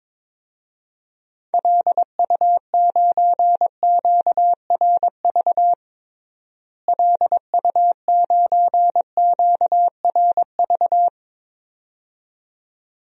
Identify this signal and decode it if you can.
Morse code, heard as a keyed tone, about 22 wpm, 700 Hz. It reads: LU9QRV LU9QRV
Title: DXCC-100 Call Signs - 6 Characters - Repeated 2x - 22wpm